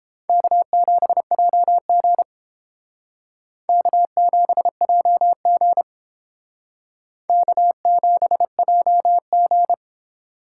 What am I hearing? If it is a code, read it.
Morse code, heard as a keyed tone, about 30 wpm, 700 Hz. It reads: K7JG K7JG K7JG